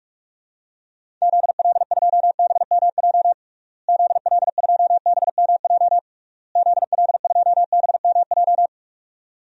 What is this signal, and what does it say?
Morse code, heard as a keyed tone, about 46 wpm, 700 Hz. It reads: ZL2BMJ ZL2BMJ ZL2BMJ
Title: DXCC-100 Call Signs - 6 Characters - Repeated 3x - 45wpm